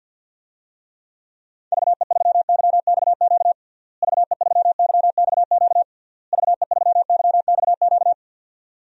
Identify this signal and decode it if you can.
Morse code, heard as a keyed tone, about 51 wpm, 700 Hz. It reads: VE3YXQ VE3YXQ VE3YXQ
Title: DXCC-100 Call Signs - 6 Characters - Repeated 3x - 50wpm